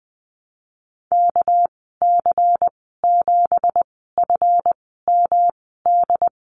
Morse code, heard as a keyed tone, about 20 wpm, spaced as for 15 wpm, 700 Hz. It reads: KC7FMD
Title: 219 Lesson 35g - Number 0 - Call Signs 20wpm